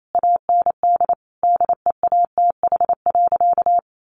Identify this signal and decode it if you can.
Morse code, heard as a keyed tone, about 28 wpm, 700 Hz. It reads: AND DEATH.
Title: Sentences from Top 1100 Words 28wpm